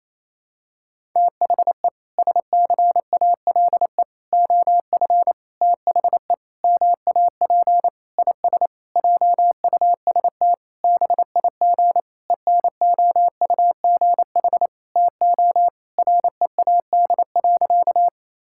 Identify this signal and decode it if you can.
Morse code, heard as a keyed tone, about 28 wpm, 700 Hz. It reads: THE SCALE OF THE MAP IS JUST BIG ENOUGH TO READ.
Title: Sentences from Top 1000 Words - Review Part 5 - 28wpm